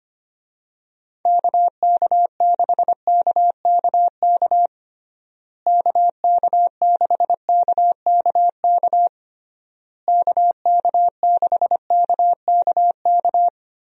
Morse code, heard as a keyed tone, about 25 wpm, 700 Hz. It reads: KK6KKK KK6KKK KK6KKK
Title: USA Call Signs - 6 Characters - Repeated 3x - 25wpm